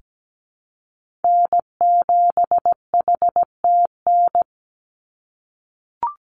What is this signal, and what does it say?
Morse code, heard as a keyed tone, about 17 wpm, 700 Hz. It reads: N7HTN
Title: USA Call Signs 17wpm